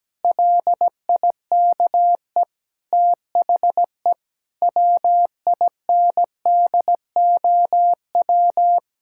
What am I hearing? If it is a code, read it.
Morse code, heard as a keyed tone, about 17 wpm, 700 Hz. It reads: LIKE THE WINDOW
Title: Sentences from Top 1600 Words 17wpm